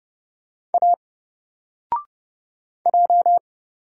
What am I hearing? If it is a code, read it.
Morse code, heard as a keyed tone, about 31 wpm, 700 Hz. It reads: A J
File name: Single-Letter-Number-30wpm